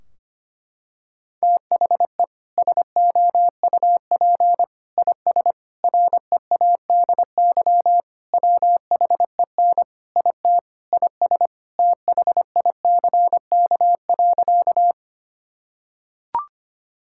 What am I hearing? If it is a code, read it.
Morse code, heard as a keyed tone, about 25 wpm, 700 Hz. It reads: THE SOUP IS READY WHEN IT IS THICK.
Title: Sentences from Top 1600 Words 25wpm